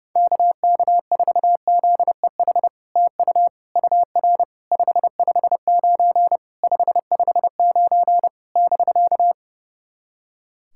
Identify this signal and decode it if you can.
Morse code, heard as a keyed tone, about 32 wpm, 700 Hz. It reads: KK4ZEH TU UR 559 559 <BK>
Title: SOTA QSO Practice 30wpm